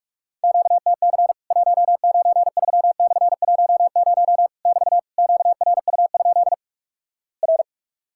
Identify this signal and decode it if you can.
Morse code, heard as a keyed tone, about 44 wpm, 700 Hz. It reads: QTC 193/10 <BT> QRU? R
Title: Worked All Europe DX Contest - QTC Exchange - 45wpm